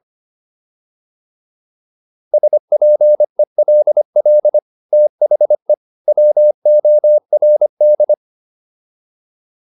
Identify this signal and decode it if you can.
Morse code, heard as a keyed tone, about 25 wpm, 600 Hz. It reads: SPELL THE WORD